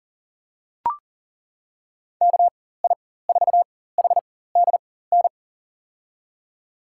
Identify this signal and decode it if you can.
Morse code, heard as a keyed tone, about 40 wpm, spaced as for 19 wpm, 700 Hz. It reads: KI4HDN
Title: USA Call Signs - 6 Characters - 4x Character Spacing - 40wpm